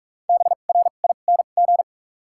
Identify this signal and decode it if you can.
Morse code, heard as a keyed tone, about 47 wpm, spaced as for 33 wpm, 700 Hz. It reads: BRING
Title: 2x Character Spacing - Top 500 Words - 45wpm